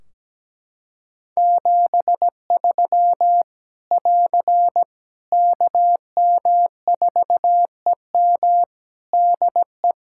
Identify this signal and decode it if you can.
Morse code, heard as a keyed tone, about 17 wpm, 700 Hz. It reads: 73 <AR> KM4EM DE
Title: Practice QSO - Exchange 4 - Ending - 17wpm